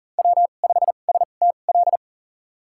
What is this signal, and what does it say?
Morse code, heard as a keyed tone, about 40 wpm, spaced as for 29 wpm, 700 Hz. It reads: W5STL
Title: USA Call Signs - 5 Characters - 2x Character Spacing - 40wpm